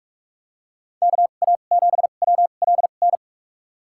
Morse code, heard as a keyed tone, about 46 wpm, spaced as for 32 wpm, 690 Hz. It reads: KA7WLN
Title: USA Call Signs - 6 Characters - 2x Character Spacing - 45wpm